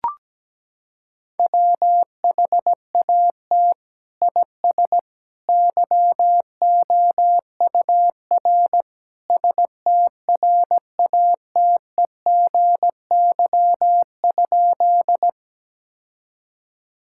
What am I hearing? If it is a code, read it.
Morse code, heard as a keyed tone, about 17 wpm, 700 Hz. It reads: WHAT IS YOUR STRATEGY?
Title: Sentences from Top 1000 Words - Review Part 5 - 17wpm